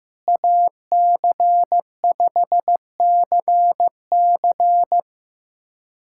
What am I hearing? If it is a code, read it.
Morse code, heard as a keyed tone, about 15 wpm, 700 Hz. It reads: AC5CC
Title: USA Callsign DE Callsign 15wpm